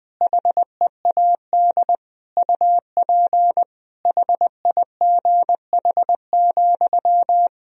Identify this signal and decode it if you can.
Morse code, heard as a keyed tone, about 21 wpm, 700 Hz. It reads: HEAD UP HIGH,
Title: Sentences from Top 1000 Words - Review Part 2 - 20wpm